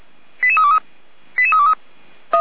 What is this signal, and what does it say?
Five-tone selective call (ZVEI), heard as quick runs of five tones, tones of about 70 ms. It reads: 88323 88323
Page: audio.f85360.feuerwehr-orlamuende.org